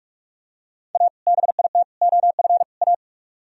Morse code, heard as a keyed tone, about 44 wpm, 700 Hz. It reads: A BIT OF A